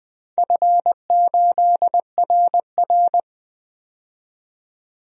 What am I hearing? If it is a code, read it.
Morse code, heard as a keyed tone, about 20 wpm, 700 Hz. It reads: F8RR
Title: DXCC-100 Call Signs - 4 Characters - 20wpm